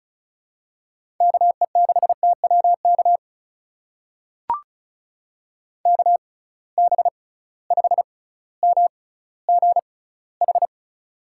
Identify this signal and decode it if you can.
Morse code, heard as a keyed tone, about 36 wpm, spaced as for 18 wpm, 700 Hz. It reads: KE6TWK KB5MGH
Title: USA Call Signs - 6 Characters - 6x Character Spacing - 35wpm